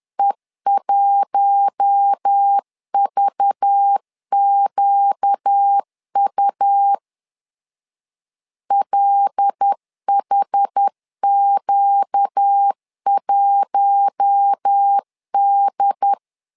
Morse code, heard as a keyed tone, about 10 wpm, 800 Hz. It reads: E1VQU LHQ1D
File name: LEZIONI / 5 LBVC / 5 LBV 10wpm 2248